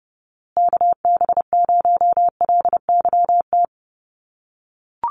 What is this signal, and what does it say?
Morse code, heard as a keyed tone, about 30 wpm, 700 Hz. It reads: KB0LYT
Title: USA Call Signs Repeated 3x 30wpm